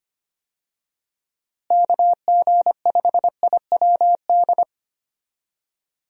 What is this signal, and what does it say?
Morse code, heard as a keyed tone, about 25 wpm, 700 Hz. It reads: KG5IWD
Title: USA Call Signs - 6 Characters - Repeated 2x - 25wpm